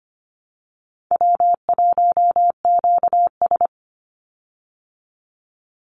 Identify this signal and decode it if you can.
Morse code, heard as a keyed tone, about 25 wpm, 700 Hz. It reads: W1QS
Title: Easy Call Signs Repeated 2x - 25wpm